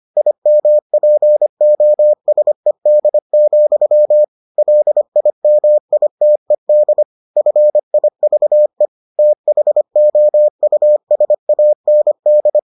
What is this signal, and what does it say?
Morse code, heard as a keyed tone, about 25 wpm, 600 Hz. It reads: IMPOSED, LIMITED FIVE THOUSAND